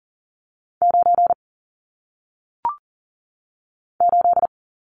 Morse code, heard as a keyed tone, about 40 wpm, 700 Hz. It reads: 9 8
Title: Numbers - 1-Digit 40wpm